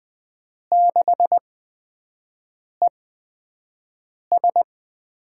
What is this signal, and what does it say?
Morse code, heard as a keyed tone, about 20 wpm, spaced as for 5 wpm, 700 Hz. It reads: 6ES
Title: USA Call Signs - 5 Characters - 8x Character Spacing - 20wpm